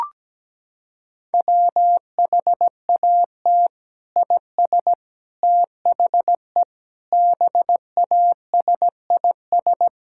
Morse code, heard as a keyed tone, about 17 wpm, 700 Hz. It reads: WHAT IS THE BASIS
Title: Sentences from Top 1000 Words - Review Part 5 - 17wpm